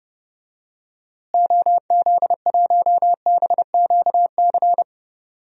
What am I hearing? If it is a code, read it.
Morse code, heard as a keyed tone, about 30 wpm, 700 Hz. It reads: OZ1BQC
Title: DXCC-100 Call Signs - 6 Characters - Repeated 2x - 30wpm